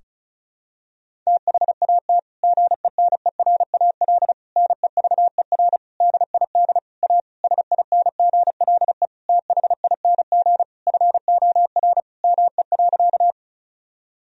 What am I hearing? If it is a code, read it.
Morse code, heard as a keyed tone, about 36 wpm, 700 Hz. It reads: THAT GENERAL NEVER DID A SINGLE THING FOR ME.
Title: Sentences from Top 1000 Words - Review Part 4 - 35wpm